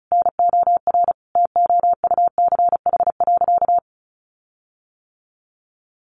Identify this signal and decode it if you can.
Morse code, heard as a keyed tone, about 34 wpm, 700 Hz. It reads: NOR TOUCH.
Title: Sentences from Top 1500 Words 35wpm